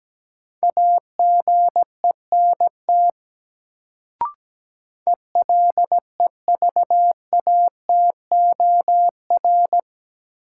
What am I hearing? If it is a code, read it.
Morse code, heard as a keyed tone, about 17 wpm, 700 Hz. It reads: AGENT ELEVATOR
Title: Sentences from Top 1600 Words - New Word Drill Alt - 17wpm